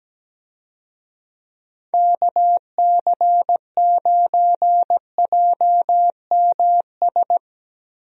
Morse code, heard as a keyed tone, about 17 wpm, 700 Hz. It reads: KC9JMS
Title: USA Call Signs - 6 Characters - Repeated 2x - 17wpm